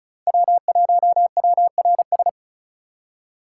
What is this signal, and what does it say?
Morse code, heard as a keyed tone, about 44 wpm, 700 Hz. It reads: W1WRS
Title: Speed Racing - USA 1x3 Call Signs 35wpm